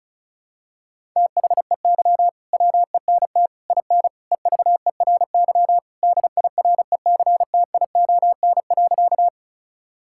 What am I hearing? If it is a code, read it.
Morse code, heard as a keyed tone, about 34 wpm, 700 Hz. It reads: THEY WENT IN EVERY DIRECTION.
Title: Sentences from Top 1000 Words - Review Part 5 - 35wpm